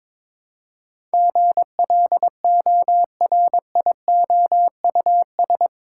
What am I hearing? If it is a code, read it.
Morse code, heard as a keyed tone, about 22 wpm, 700 Hz. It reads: GLORIOUS
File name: practice / 22wpm / Eight Letter Words - 22wpm